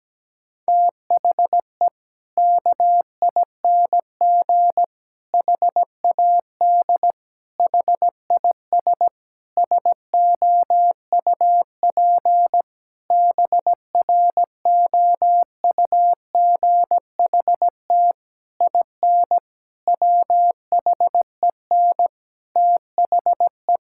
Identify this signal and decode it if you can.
Morse code, heard as a keyed tone, about 17 wpm, 700 Hz. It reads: THE KING HAD HIS SOUP BROUGHT IN WHEN THE
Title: Sentences from Top 1000 Words - Review Part 6 - 17wpm